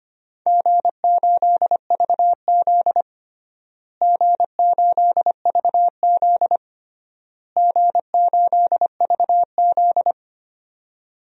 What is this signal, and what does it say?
Morse code, heard as a keyed tone, about 25 wpm, 700 Hz. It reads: G8VZ G8VZ G8VZ
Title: DXCC-100 Call Signs - 4 Characters - Repeated 3x - 25wpm